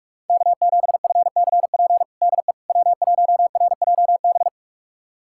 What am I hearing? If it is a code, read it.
Morse code, heard as a keyed tone, about 47 wpm, 700 Hz. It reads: K7FCP DE W1RJB
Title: USA Callsign DE Callsign - 5 Characters - 45wpm